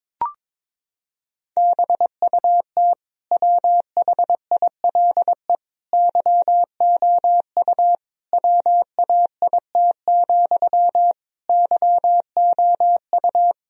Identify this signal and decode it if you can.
Morse code, heard as a keyed tone, about 22 wpm, 700 Hz. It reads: BUT WHILE YOU WAIT, YOU
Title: Sentences from Top 1000 Words - Review Part 1 - 22wpm